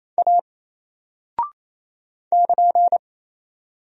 Morse code, heard as a keyed tone, about 27 wpm, 700 Hz. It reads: A <KN>
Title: Single Character (Letter:Number:Punctuation:Prosign) 28wpm